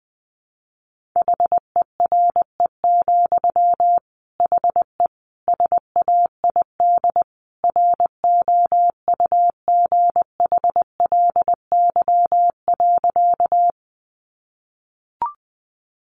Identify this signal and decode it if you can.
Morse code, heard as a keyed tone, about 20 wpm, 700 Hz. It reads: HERE, HE SAID ROUGHLY.
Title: Sentences from Top 1500 Words 20wpm